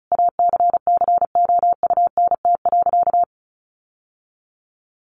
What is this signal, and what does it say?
Morse code, heard as a keyed tone, about 34 wpm, 700 Hz. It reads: ACCOUNT.
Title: Sentences from Top 1400 Words 35wpm